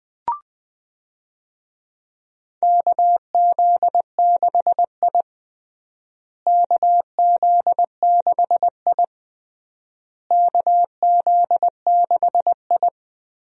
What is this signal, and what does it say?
Morse code, heard as a keyed tone, about 20 wpm, 700 Hz. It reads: KZ6I KZ6I KZ6I
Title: USA Call Signs - 4 Characters - Repeated 3x - 20wpm